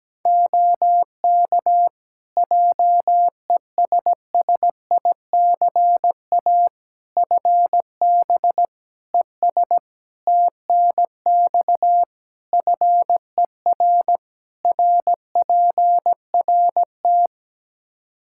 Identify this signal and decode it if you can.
Morse code, heard as a keyed tone, about 17 wpm, 700 Hz. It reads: OK JESSICA FB ES TNX FER RPRT
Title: Practice QSO - Exchange 2 - Info - 17wpm